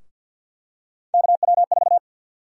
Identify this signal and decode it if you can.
Morse code, heard as a keyed tone, about 47 wpm, 700 Hz. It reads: KW4
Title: USA Call Sign Prefixes - Repeated 2x - 50wpm